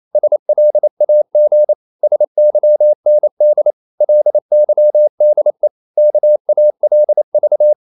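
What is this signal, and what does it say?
Morse code, heard as a keyed tone, about 28 wpm, 600 Hz. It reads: SLAG SYND LYDE KALV
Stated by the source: Danish short words sent in Morse at 28 wpm, from CW med Gnister.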